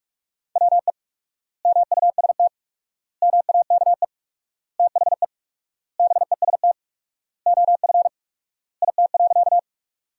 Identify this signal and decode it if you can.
Morse code, heard as a keyed tone, about 44 wpm, 700 Hz. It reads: WE MUST MAKE THE BEST OF IT.
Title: 4x Word-Spacing - Sentences from Top 400 Words - 45wpm